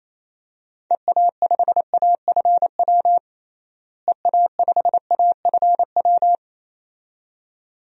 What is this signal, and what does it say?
Morse code, heard as a keyed tone, about 28 wpm, 700 Hz. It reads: EA5AFW EA5AFW
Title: DXCC-100 Call Signs - 6 Characters - Repeated 2x - 28wpm